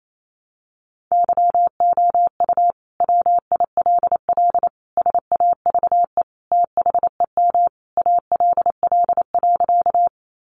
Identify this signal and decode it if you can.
Morse code, heard as a keyed tone, about 28 wpm, 700 Hz. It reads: YOU WILL SAVE THEM ALL.
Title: Sentences from Top 1100 Words 28wpm